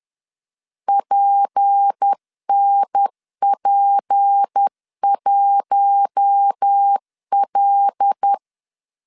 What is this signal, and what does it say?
Morse code, heard as a keyed tone, about 10 wpm, 800 Hz. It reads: PNP1L
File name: CW / DET / 1 6 1500C S C 10wpm 4252